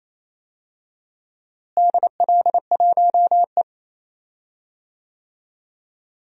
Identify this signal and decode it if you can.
Morse code, heard as a keyed tone, about 28 wpm, 700 Hz. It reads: DL1E
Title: DXCC-100 Call Signs - 4 Characters - Repeated 2x - 28wpm